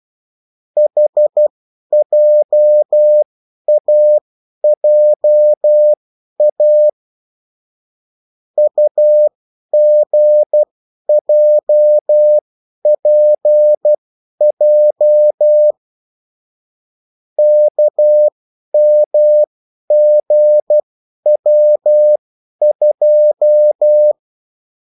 Morse code, heard as a keyed tone, about 12 wpm, spaced as for 10 wpm, 600 Hz. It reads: HJAJA UGJPJ KMGW2
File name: morse mp3 / revision9-vitesse 10 mots